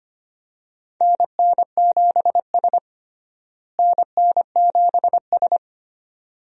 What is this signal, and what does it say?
Morse code, heard as a keyed tone, about 25 wpm, 700 Hz. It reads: NN7S NN7S
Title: USA Call Signs - 4 Characters - Repeated 2x - 25wpm